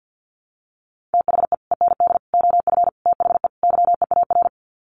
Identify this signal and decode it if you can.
Morse code, heard as a keyed tone, about 48 wpm, 700 Hz. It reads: THE END OF THE YEAR
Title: Sets of 5 Words 50wpm